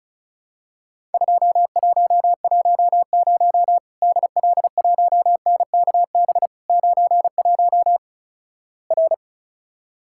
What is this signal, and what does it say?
Morse code, heard as a keyed tone, about 36 wpm, 700 Hz. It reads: 2110 DL1NKB 91 R
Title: Worked All Europe DX Contest - QTC Exchange - 35wpm